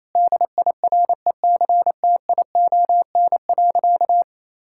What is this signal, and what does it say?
Morse code, heard as a keyed tone, about 27 wpm, 700 Hz. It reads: DIRECTION.
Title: Sentences from Top 1000 Words - Review Part 5 - 28wpm